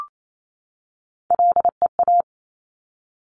Morse code, heard as a keyed tone, about 28 wpm, 700 Hz. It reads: LEA
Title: Top Three Letter Combinations in Words 28wpm